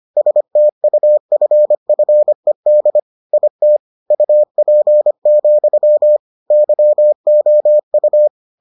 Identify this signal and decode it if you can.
Morse code, heard as a keyed tone, about 25 wpm, 600 Hz. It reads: STUFFED IT UP, YOU